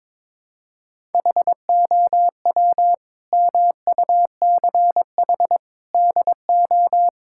Morse code, heard as a keyed tone, about 22 wpm, 700 Hz. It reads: HOW MUCH DO
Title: Sentences from Top 1000 Words - Review Part 6 - 22wpm